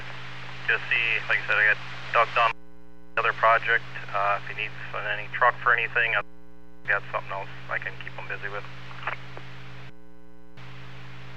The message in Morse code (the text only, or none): none